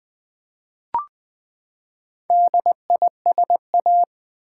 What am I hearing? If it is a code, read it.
Morse code, heard as a keyed tone, about 20 wpm, 700 Hz. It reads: DISA
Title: Build A Word - 4 Letters At A Time - 12-Letter Words - 20wpm